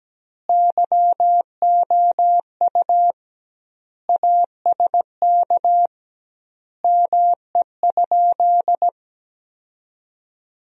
Morse code, heard as a keyed tone, about 17 wpm, 700 Hz. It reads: YOU ASK ME?
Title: 2x Word-Spacing - Sentences from Top 200 Words - 17wpm